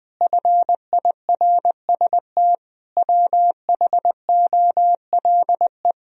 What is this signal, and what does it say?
Morse code, heard as a keyed tone, about 21 wpm, 700 Hz. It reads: FIRST WHOLE
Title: Sentences from Top 1000 Words - Review Part 1 - 20wpm